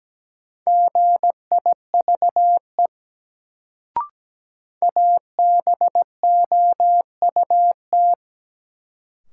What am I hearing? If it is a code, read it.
Morse code, heard as a keyed tone, about 17 wpm, 700 Hz. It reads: GIVE ABOUT
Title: Sets of 2 Words - New Word Drill - 17wpm